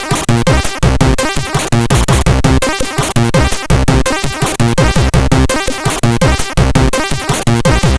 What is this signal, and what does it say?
There is no code.